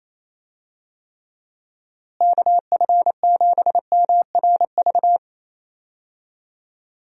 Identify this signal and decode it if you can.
Morse code, heard as a keyed tone, about 28 wpm, 700 Hz. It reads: KF7MRV
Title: USA Call Signs - 6 Characters - Repeated 2x - 28wpm